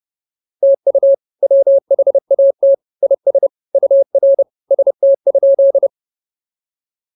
Morse code, heard as a keyed tone, about 30 wpm, 550 Hz. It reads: TU WHAT IS UR ST?